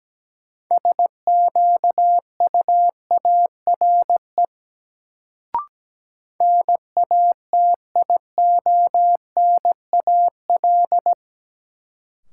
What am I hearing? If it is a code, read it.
Morse code, heard as a keyed tone, about 17 wpm, 700 Hz. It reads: SQUARE NATIONAL
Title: Sentences from Top 1200 Words - New Word Drill Alt - 17wpm